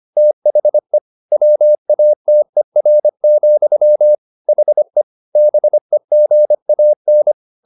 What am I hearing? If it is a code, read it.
Morse code, heard as a keyed tone, about 25 wpm, 600 Hz. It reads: THE WATER, HE BEGAN
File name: the old man and the sea part 1 25wpm